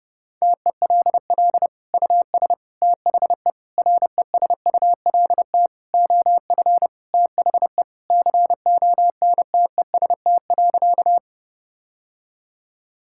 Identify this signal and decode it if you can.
Morse code, heard as a keyed tone, about 30 wpm, 700 Hz. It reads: TELL US THE RESULT OF THE CONTEST.
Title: Sentences from Top 1600 Words 30wpm